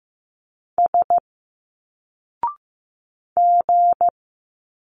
Morse code, heard as a keyed tone, about 15 wpm, 700 Hz. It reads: S G
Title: Single Letters 15wpm